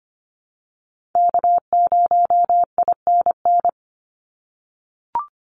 Morse code, heard as a keyed tone, about 25 wpm, 700 Hz. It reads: K0INN
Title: Call Signs - 25wpm